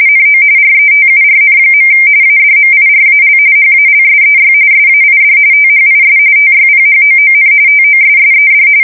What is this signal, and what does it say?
RTTY, about 45.45 baud. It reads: NO RETURN. ADMIRAL HALSEY SAID THIS TYPHOON TOSSED
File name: fsk170 / battleshipnj